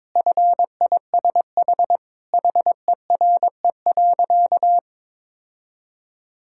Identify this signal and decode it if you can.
Morse code, heard as a keyed tone, about 22 wpm, 700 Hz. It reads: FISH HERE.
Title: Sentences from Top 1000 Words - Review Part 6 - 22wpm